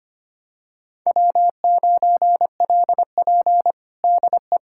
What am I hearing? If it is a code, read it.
Morse code, heard as a keyed tone, about 25 wpm, 700 Hz. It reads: W9LP DE
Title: USA Callsign DE Callsign - 4 Characters - 25wpm